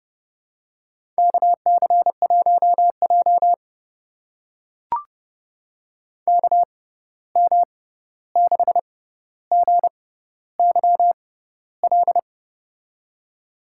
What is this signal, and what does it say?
Morse code, heard as a keyed tone, about 32 wpm, spaced as for 14 wpm, 700 Hz. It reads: KC1J KM6GYL
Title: USA Call Signs - 6x Character Spacing - 30wpm